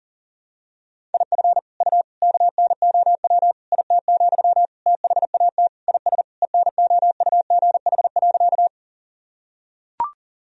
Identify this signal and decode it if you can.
Morse code, heard as a keyed tone, about 37 wpm, 700 Hz. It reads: IF U KNOW IT, THAT IS ENOUGH.
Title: Sentences from Top 400 Words and Abbreviations 40wpm